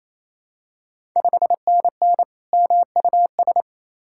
Morse code, heard as a keyed tone, about 28 wpm, 700 Hz. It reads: 5NN MUS
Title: Ontario QSO Party Exchange 28wpm